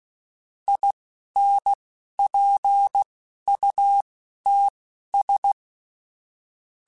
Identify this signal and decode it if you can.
Morse code, heard as a keyed tone, about 16 wpm, spaced as for 11 wpm, 800 Hz. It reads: INPUTS
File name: Technical Subjects / Technical Subjects-mp3s / AND Gate @10